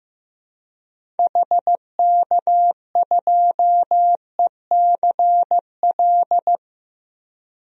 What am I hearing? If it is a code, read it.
Morse code, heard as a keyed tone, about 15 wpm, 700 Hz. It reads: HK2ECL
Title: DXCC-100 Call Signs - 6 Characters - 15wpm